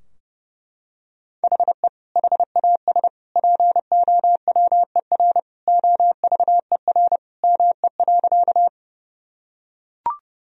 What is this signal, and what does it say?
Morse code, heard as a keyed tone, about 30 wpm, 700 Hz. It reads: HE HAS POWER OVER ME.
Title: Sentences from Top 1000 Words - Review Part 2 - 30wpm